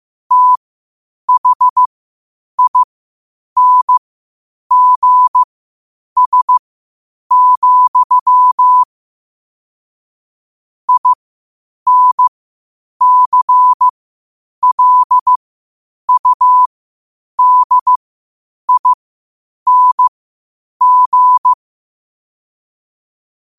Morse code, heard as a keyed tone, about 15 wpm, spaced as for 8 wpm, 1000 Hz. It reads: THINGS, INCLUDING
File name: qotdfiles / 8wpm